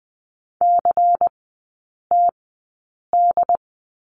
Morse code, heard as a keyed tone, about 20 wpm, spaced as for 8 wpm, 690 Hz. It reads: CTD
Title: 093 Lesson 16e - Letter c - Three Characters 20wpm